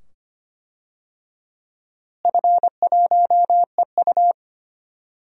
Morse code, heard as a keyed tone, about 25 wpm, 700 Hz. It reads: F1EU
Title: DXCC-100 Call Signs - 4 Characters - Repeated 2x - 25wpm